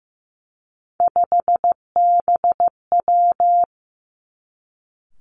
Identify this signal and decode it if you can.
Morse code, heard as a keyed tone, about 15 wpm, 700 Hz. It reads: 5BW
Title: Three Challenging Letter-Number Combinations 15wpm